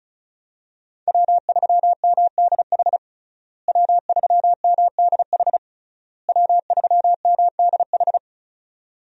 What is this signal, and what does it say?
Morse code, heard as a keyed tone, about 36 wpm, 700 Hz. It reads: W3MDH W3MDH W3MDH
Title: USA Call Signs - 5 Characters - Repeated 3x - 35wpm